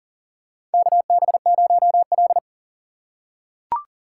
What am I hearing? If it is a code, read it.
Morse code, heard as a keyed tone, about 40 wpm, 700 Hz. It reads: KB0L
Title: USA Call Signs - 4 Characters - 3x Character Spacing - 40wpm